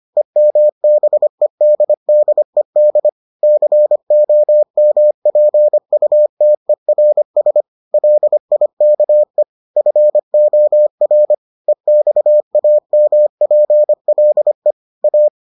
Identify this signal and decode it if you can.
Morse code, heard as a keyed tone, about 25 wpm, 600 Hz. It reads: EMBEDDED COMPUTERS LIKE FOR EXAMPLE A